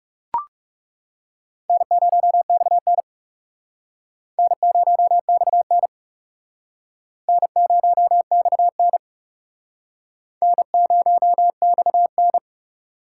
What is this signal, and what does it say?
Morse code, heard as a keyed tone, about 38 wpm, 700 Hz. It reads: N0XN N0XN N0XN N0XN